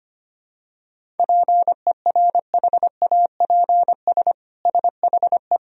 Morse code, heard as a keyed tone, about 25 wpm, 700 Hz. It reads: PERHAPS SHE
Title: Sentences from Top 1000 Words - Review Part 1 - 25wpm